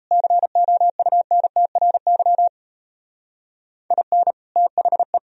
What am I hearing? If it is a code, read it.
Morse code, heard as a keyed tone, about 34 wpm, 700 Hz. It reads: COUNTRY IN THE
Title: Speed-Racing - Sets of 3 Words 28wpm